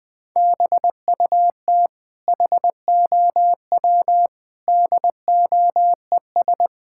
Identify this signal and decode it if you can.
Morse code, heard as a keyed tone, about 20 wpm, 700 Hz. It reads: BUT HOW DOES
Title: Sentences from Top 1000 Words - Review Part 6 - 20wpm